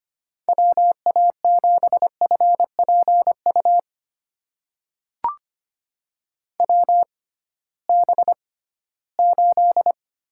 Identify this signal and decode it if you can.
Morse code, heard as a keyed tone, about 25 wpm, spaced as for 16 wpm, 700 Hz. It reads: WA7FPU WB8
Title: USA Call Signs - 6 Characters - 6x Character Spacing - 25wpm